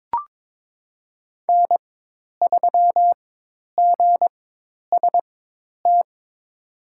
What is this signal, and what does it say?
Morse code, heard as a keyed tone, about 23 wpm, spaced as for 10 wpm, 700 Hz. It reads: N3GST
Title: USA Call Signs - 4x Character Spacing - 22wpm